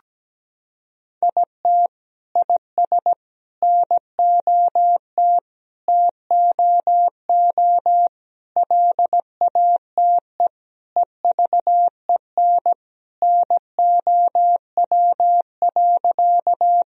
Morse code, heard as a keyed tone, about 17 wpm, 700 Hz. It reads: IT IS NOT TOO LATE EVEN NOW.
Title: Sentences from Top 1000 Words - Review Part 1 - 17wpm